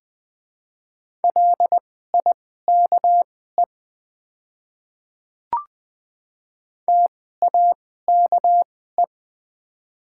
Morse code, heard as a keyed tone, about 20 wpm, spaced as for 14 wpm, 700 Hz. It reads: LIKE TAKE
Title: 2x Character Spacing - Top 200 Words - 20wpm